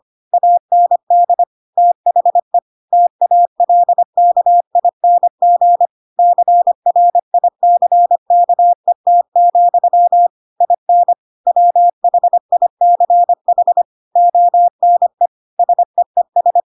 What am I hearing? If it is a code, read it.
Morse code, heard as a keyed tone, about 25 wpm, 700 Hz. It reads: AND THE TALKING CRICKET, IN WHICH ONE SEES
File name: Pinocchio / Pinocchio ch4 25 25 ik1hge